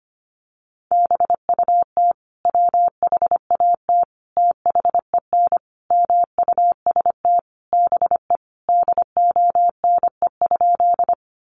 Morse code, heard as a keyed tone, about 25 wpm, 700 Hz. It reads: BUT WHAT THEN MUST BE DONE?
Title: Sentences from Top 500 Words 25wpm